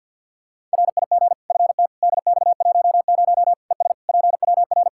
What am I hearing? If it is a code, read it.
Morse code, heard as a keyed tone, about 52 wpm, 700 Hz. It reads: RIG FTDX10 ES PWR
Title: Practice QSO - Exchange 2 - Info - 50wpm